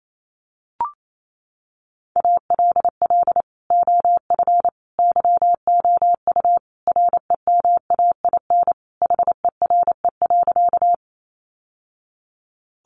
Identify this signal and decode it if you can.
Morse code, heard as a keyed tone, about 28 wpm, 700 Hz. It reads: ALL OF YOU REMAIN HERE.
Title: Sentences from Top 1200 Words 28wpm